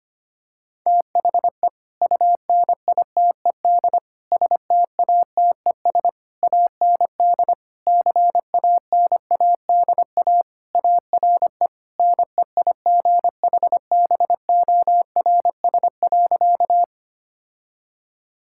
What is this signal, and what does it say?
Morse code, heard as a keyed tone, about 25 wpm, 700 Hz. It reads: THE UNITED STATES AND CANADA ARE NEIGHBORS.
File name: practice / 25wpm / Sentences with 4-Word Phrases - 25wpm